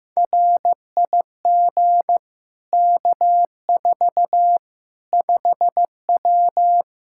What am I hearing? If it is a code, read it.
Morse code, heard as a keyed tone, about 15 wpm, 700 Hz. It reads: RIG K4 5W
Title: Practice QSO - Exchange 2 - Info - 15wpm